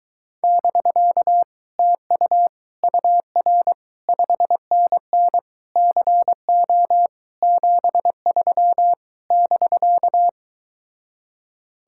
Morse code, heard as a keyed tone, about 24 wpm, 700 Hz. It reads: <BK> TU UR 5NN CO 73 <BK>